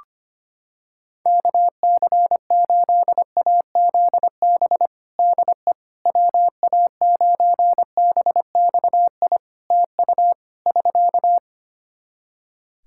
Morse code, heard as a keyed tone, about 25 wpm, 700 Hz. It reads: KC8AZB DE WA9BXI TU <SK>
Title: K1USN SST QSO Practice 25wpm